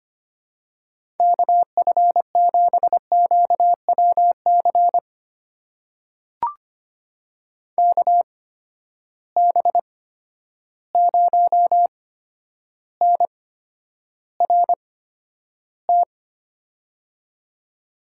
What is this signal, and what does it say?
Morse code, heard as a keyed tone, about 25 wpm, spaced as for 11 wpm, 700 Hz. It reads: KF7QWC KB0NRT
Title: USA Call Signs - 6 Characters - 8x Character Spacing - 25wpm